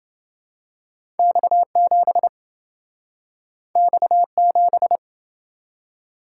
Morse code, heard as a keyed tone, about 28 wpm, 700 Hz. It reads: X7 X7